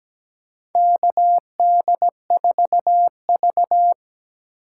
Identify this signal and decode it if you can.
Morse code, heard as a keyed tone, about 17 wpm, 700 Hz. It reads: KD4V
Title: USA Call Signs - 4 Characters - 6x Character Spacing - 17wpm